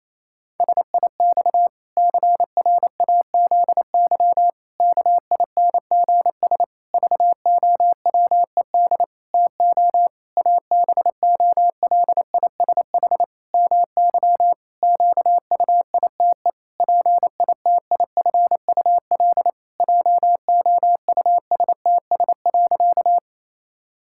Morse code, heard as a keyed tone, about 28 wpm, 700 Hz. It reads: SIX CRAZY KINGS VOWED TO ABOLISH MY QUITE PITIFUL JOUSTS.
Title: Pangram Challenge v2 - 28wpm